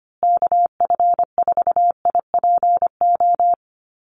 Morse code, heard as a keyed tone, about 25 wpm, 700 Hz. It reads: KF4IPO
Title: USA Call Signs Repeated 2x 25wpm